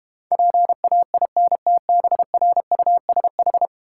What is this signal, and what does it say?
Morse code, heard as a keyed tone, about 31 wpm, 700 Hz. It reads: PAINTBRUSH
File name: Easy Words - Speed-Racing with Hints - 25wpm